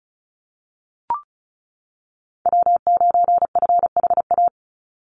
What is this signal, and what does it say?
Morse code, heard as a keyed tone, about 34 wpm, 700 Hz. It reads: W9FHA
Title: NA SOTA Chaser Callsigns 35wpm